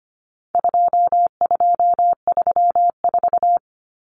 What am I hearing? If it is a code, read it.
Morse code, heard as a keyed tone, about 25 wpm, 700 Hz. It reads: 2234
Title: Common Cut Numbers - 4-Digit 25wpm